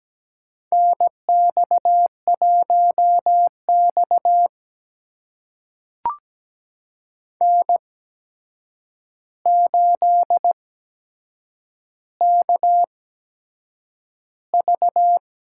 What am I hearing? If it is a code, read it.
Morse code, heard as a keyed tone, about 17 wpm, spaced as for 7 wpm, 700 Hz. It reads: NX1X N8KV
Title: USA Call Signs - 4 Characters - 8x Character Spacing - 17wpm